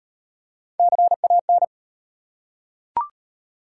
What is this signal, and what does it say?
Morse code, heard as a keyed tone, about 34 wpm, 700 Hz. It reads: CAN